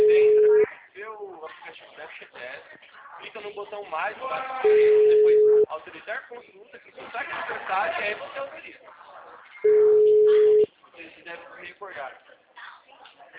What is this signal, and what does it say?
Ringback tone (Europe, 425 Hz), with speech.
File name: out- -2002- - - .28786